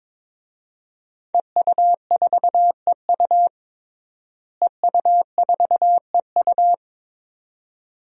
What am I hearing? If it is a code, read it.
Morse code, heard as a keyed tone, about 22 wpm, 700 Hz. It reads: EU4EU EU4EU
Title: DXCC-100 Call Signs - 5 Characters - Repeated 2x - 22wpm